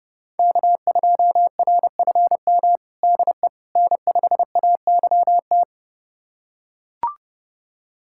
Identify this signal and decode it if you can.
Morse code, heard as a keyed tone, about 32 wpm, 700 Hz. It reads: K2RFM DE N5AYT
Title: USA Callsign DE Callsign - 5 Characters - 30wpm